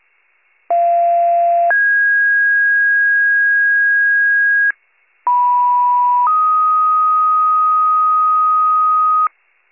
Two-tone page, heard as a one-second tone, then a three-second tone, about 690 then 1650 Hz, about 980 then 1250 Hz, with noise. It is heard from 0.7 to 9.3 s.